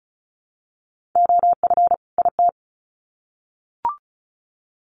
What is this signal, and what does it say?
Morse code, heard as a keyed tone, about 34 wpm, 700 Hz. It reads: OF IT
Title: Sets of 2 Words 35wpm